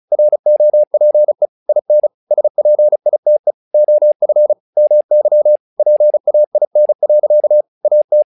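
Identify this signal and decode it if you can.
Morse code, heard as a keyed tone, about 34 wpm, 600 Hz. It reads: ROPE IN SPITE OF MY PAIN. AT
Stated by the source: Morse practice text "Tower Trouble" sent at 35 WPM, from the SKCC - Straight Key Century Club learning center audio files.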